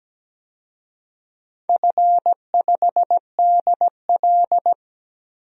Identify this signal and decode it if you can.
Morse code, heard as a keyed tone, about 17 wpm, 700 Hz. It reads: F5DL
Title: DXCC-100 Call Signs - 4 Characters - 17wpm